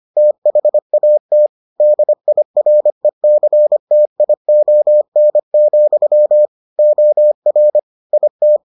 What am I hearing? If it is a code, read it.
Morse code, heard as a keyed tone, about 25 wpm, 600 Hz. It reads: THAT DIRECTION, OR IT